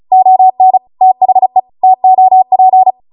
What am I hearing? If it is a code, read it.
Morse code, heard as a keyed tone, about 36 wpm, 750 Hz. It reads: ON THE TOP